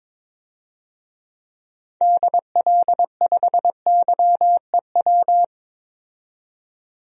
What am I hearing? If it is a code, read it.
Morse code, heard as a keyed tone, about 21 wpm, 700 Hz. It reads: DL5YEW
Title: DXCC-100 Call Signs - 6 Characters - 22wpm